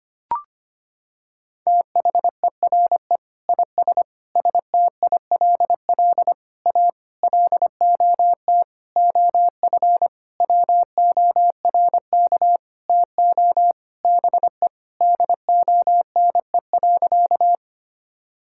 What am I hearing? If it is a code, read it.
Morse code, heard as a keyed tone, about 25 wpm, 700 Hz. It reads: THERE IS STILL A LOT OF WORK TO BE DONE.
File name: practice / 25wpm / Sentences with 9-Word Phrases - 25wpm